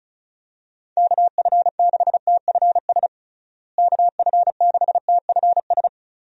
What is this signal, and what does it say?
Morse code, heard as a keyed tone, about 34 wpm, 700 Hz. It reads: KF6TFS KF6TFS
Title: USA Call Signs - 6 Characters - Repeated 2x - 35wpm